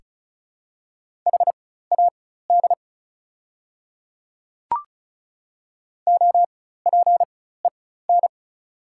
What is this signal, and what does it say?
Morse code, heard as a keyed tone, about 34 wpm, spaced as for 16 wpm, 690 Hz. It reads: HAD OPEN
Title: 4x Character Spacing - Top 300 Words - 35wpm